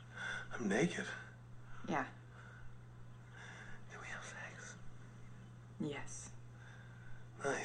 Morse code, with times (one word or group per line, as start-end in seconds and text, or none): none